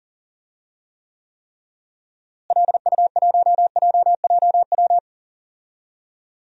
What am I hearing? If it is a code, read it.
Morse code, heard as a keyed tone, about 40 wpm, 700 Hz. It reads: LU1JJW
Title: DXCC-100 Call Signs - 6 Characters - Repeated 3x - 40wpm